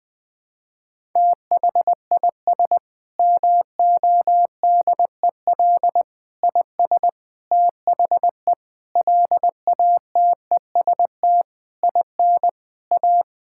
Morse code, heard as a keyed tone, about 20 wpm, 700 Hz. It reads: THIS MODEL IS THE LATEST IN A
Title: Sentences with 8-Word Phrases - Spoken Beginning - 20wpm